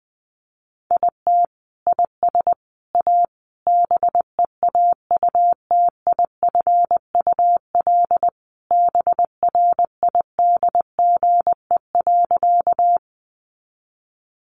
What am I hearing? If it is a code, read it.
Morse code, heard as a keyed tone, about 20 wpm, 700 Hz. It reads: IT IS A BEAUTIFUL BRIDGE.
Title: Sentences from Top 1200 Words 20wpm